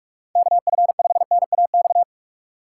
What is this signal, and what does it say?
Morse code, heard as a keyed tone, about 46 wpm, 700 Hz. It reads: KF5NAX
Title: USA Call Signs - 6 Characters - 6x Character Spacing - 45wpm